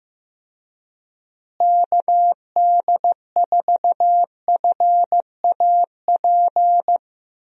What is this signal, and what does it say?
Morse code, heard as a keyed tone, about 15 wpm, 700 Hz. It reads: KD4FAP
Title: USA Call Signs - 6 Characters - Repeated 2x - 15wpm